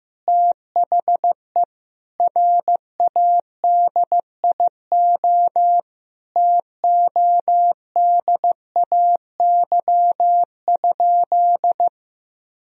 Morse code, heard as a keyed tone, about 15 wpm, 700 Hz. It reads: THE RADIO TODAY?
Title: Sentences from Top 1000 Words - Review Part 3 - 15wpm